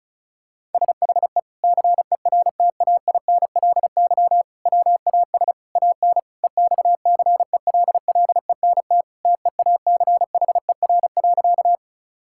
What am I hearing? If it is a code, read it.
Morse code, heard as a keyed tone, about 34 wpm, 700 Hz. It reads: SHE CERTAINLY WAS AN EXCELLENT TEACHER.
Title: Sentences from Top 1000 Words - Review Part 6 - 35wpm